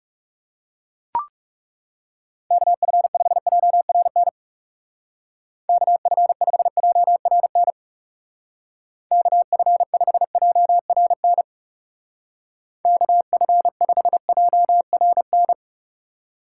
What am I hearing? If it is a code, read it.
Morse code, heard as a keyed tone, about 38 wpm, 700 Hz. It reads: KF5JRN KF5JRN KF5JRN KF5JRN